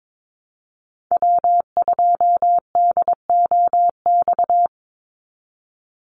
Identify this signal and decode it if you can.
Morse code, heard as a keyed tone, about 22 wpm, 700 Hz. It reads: W2DOX
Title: USA Call Signs 22wpm